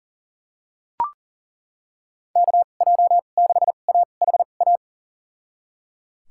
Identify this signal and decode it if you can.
Morse code, heard as a keyed tone, about 37 wpm, spaced as for 28 wpm, 700 Hz. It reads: KJ6AHA
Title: USA Call Signs - 6 Characters - 2x Character Spacing - 40wpm